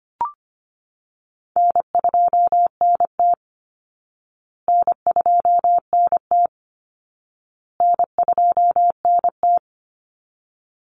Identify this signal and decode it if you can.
Morse code, heard as a keyed tone, about 25 wpm, 700 Hz. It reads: N2NT N2NT N2NT
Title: Easy Call Signs Repeated 3x - 25wpm